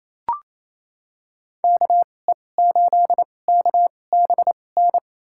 Morse code, heard as a keyed tone, about 28 wpm, spaced as for 20 wpm, 700 Hz. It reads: KE8KBN
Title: USA Call Signs - 2x Character Spacing - 28wpm